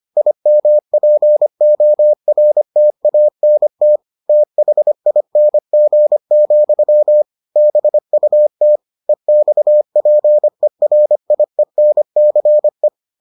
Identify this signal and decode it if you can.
Morse code, heard as a keyed tone, about 25 wpm, 600 Hz. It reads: IMPORTANT THING, BUT EXPERIENCE